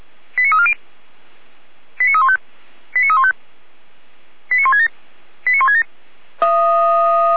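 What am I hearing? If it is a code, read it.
Five-tone selective call (ZVEI), heard as quick runs of five tones, tones of about 80 ms. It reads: 89340 89315 89315 89167 89167
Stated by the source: ZVEI five-tone call received by SDR, from the feuerwehr-orlamuende.org audio server.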